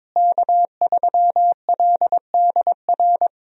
Morse code, heard as a keyed tone, about 22 wpm, 700 Hz. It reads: K3LDR
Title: USA Call Signs - 5 Characters - Repeated 3x - 22wpm